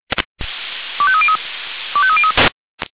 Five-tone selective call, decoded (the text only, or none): 25303 25303